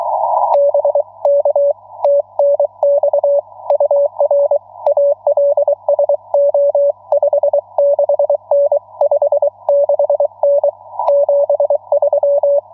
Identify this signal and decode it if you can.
Morse code, heard as a keyed tone, about 23 wpm, 570 Hz. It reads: B K TNX UR ALSO 56N 56N 73
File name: n6tr / la3anaAtYL2SM